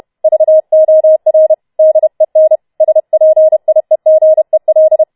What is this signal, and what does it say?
Morse code, heard as a keyed tone, about 31 wpm, 630 Hz. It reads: VOR DEN SPIEGEL